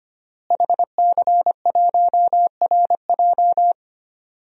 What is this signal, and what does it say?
Morse code, heard as a keyed tone, about 25 wpm, 700 Hz. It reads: HC1RJ
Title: DXCC-100 Call Signs - 5 Characters - Repeated 3x - 25wpm